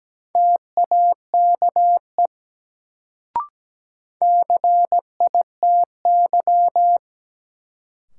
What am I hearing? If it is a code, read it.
Morse code, heard as a keyed tone, about 17 wpm, 700 Hz. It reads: TAKE CITY
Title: Sets of 2 Words - New Word Drill - 17wpm